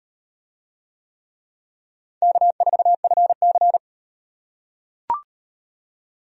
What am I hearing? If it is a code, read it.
Morse code, heard as a keyed tone, about 38 wpm, 700 Hz. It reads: K4FC